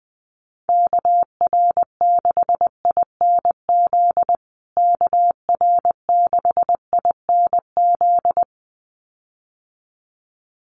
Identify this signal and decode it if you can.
Morse code, heard as a keyed tone, about 20 wpm, 700 Hz. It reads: KR6INZ KR6INZ
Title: USA Call Signs Repeated 2x 20wpm